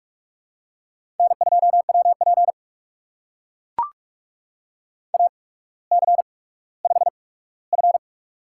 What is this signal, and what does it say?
Morse code, heard as a keyed tone, about 42 wpm, spaced as for 19 wpm, 700 Hz. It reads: N2WP AC5F
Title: USA Call Signs - 4 Characters - 8x Character Spacing - 45wpm